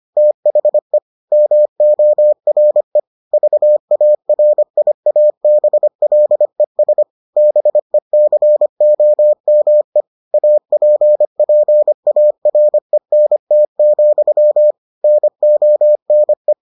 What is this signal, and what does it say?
Morse code, heard as a keyed tone, about 25 wpm, 600 Hz. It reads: THE MORE VARIABLES BECOME APPARENT, NONE